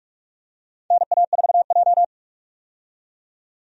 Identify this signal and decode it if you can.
Morse code, heard as a keyed tone, about 46 wpm, 700 Hz. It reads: NA4J
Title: USA Call Signs - 4 Characters - 45wpm